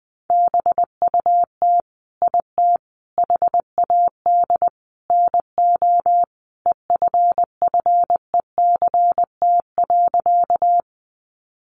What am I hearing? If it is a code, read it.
Morse code, heard as a keyed tone, about 20 wpm, 700 Hz. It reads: BUT IT HAD NO EFFECT.
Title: Sentences from Top 1000 Words 20wpm